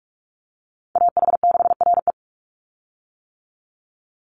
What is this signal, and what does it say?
Morse code, heard as a keyed tone, about 44 wpm, 700 Hz. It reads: AH6RE
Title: Call Signs - 45wpm